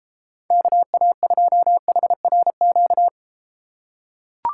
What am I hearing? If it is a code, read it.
Morse code, heard as a keyed tone, about 32 wpm, 700 Hz. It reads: KA2HRQ